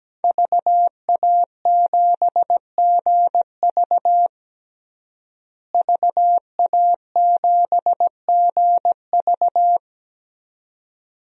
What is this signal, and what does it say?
Morse code, heard as a keyed tone, about 17 wpm, 700 Hz. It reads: VA7GV VA7GV
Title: DXCC-100 Call Signs - 5 Characters - Repeated 2x - 17wpm